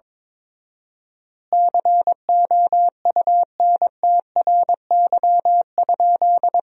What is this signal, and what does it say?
Morse code, heard as a keyed tone, about 22 wpm, 700 Hz. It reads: COUNTRY?
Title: 4x Word-Spacing - Sentences from Top 300 Words - 22wpm